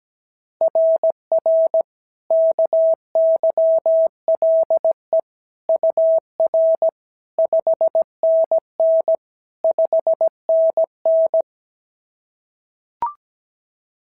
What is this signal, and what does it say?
Morse code, heard as a keyed tone, about 16 wpm, 650 Hz. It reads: RR KYLE UR 5NN 5NN
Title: SOTA QSO Practice 17wpm